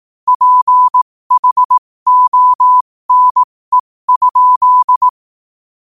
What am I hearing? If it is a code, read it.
Morse code, heard as a keyed tone, about 18 wpm, spaced as for 15 wpm, 1000 Hz. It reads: PHONE?
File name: qotdfiles / 15wpm